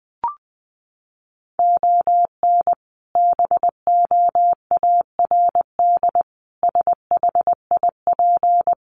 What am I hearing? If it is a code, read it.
Morse code, heard as a keyed tone, about 20 wpm, 700 Hz. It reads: ON BOARD SHIP
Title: Sentences from Top 1400 Words 20wpm